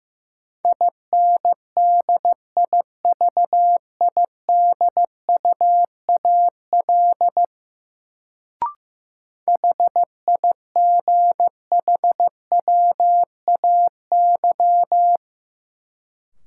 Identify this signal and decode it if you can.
Morse code, heard as a keyed tone, about 16 wpm, 700 Hz. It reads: INDIVIDUAL HIGHWAY
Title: Sentences from Top 1400 Words - New Word Drill Alt - 15wpm